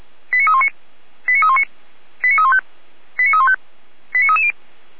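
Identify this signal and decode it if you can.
Five-tone selective call (ZVEI), heard as quick runs of five tones, tones of about 80 ms. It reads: 89319 89319 89315 89315 89330
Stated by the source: ZVEI five-tone selective call 89125, received via SDR on channel 455.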